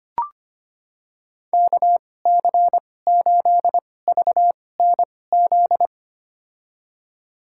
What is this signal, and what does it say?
Morse code, heard as a keyed tone, about 25 wpm, spaced as for 18 wpm, 700 Hz. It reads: KC8VNZ
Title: USA Call Signs - 6 Characters - 2x Character Spacing - 25wpm